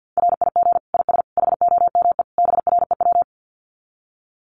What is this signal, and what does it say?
Morse code, heard as a keyed tone, about 48 wpm, 700 Hz. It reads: RIG IS HOME BREW
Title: Common QSO Phrases 50wpm